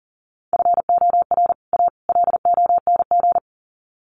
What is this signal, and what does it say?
Morse code, heard as a keyed tone, about 37 wpm, 700 Hz. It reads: FOR A LONG
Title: Sets of 3 Words 40wpm